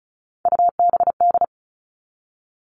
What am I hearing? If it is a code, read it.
Morse code, heard as a keyed tone, about 34 wpm, 700 Hz. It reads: UBD
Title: All Cut Numbers - 3-Digit 35wpm